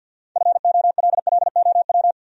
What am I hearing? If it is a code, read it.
Morse code, heard as a keyed tone, about 66 wpm, 700 Hz. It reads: FOLLOW